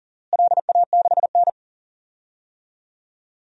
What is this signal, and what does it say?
Morse code, heard as a keyed tone, about 37 wpm, 700 Hz. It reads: LA6N
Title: DXCC-100 Call Signs - 4 Characters - Repeated 3x - 40wpm